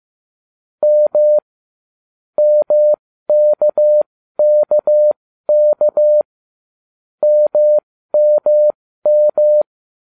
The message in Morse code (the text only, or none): M MKKK MMM